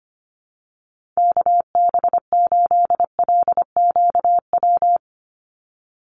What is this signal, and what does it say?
Morse code, heard as a keyed tone, about 25 wpm, 690 Hz. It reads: KB8LQW
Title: USA Call Signs 25wpm